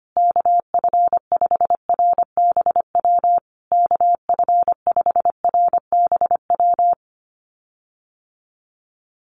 Morse code, heard as a keyed tone, about 25 wpm, 700 Hz. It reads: KF5RBW KF5RBW
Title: USA Call Signs Repeated 3x 25wpm